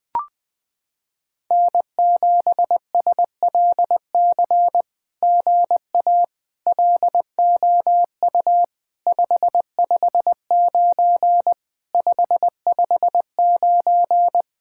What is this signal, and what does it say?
Morse code, heard as a keyed tone, about 20 wpm, 700 Hz. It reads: N7SLC GA LOU 559 559
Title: SOTA QSO Practice 20wpm